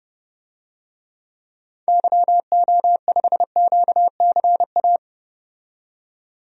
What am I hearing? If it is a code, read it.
Morse code, heard as a keyed tone, about 30 wpm, 700 Hz. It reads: YO5QCA
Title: DXCC-100 Call Signs - 6 Characters - 30wpm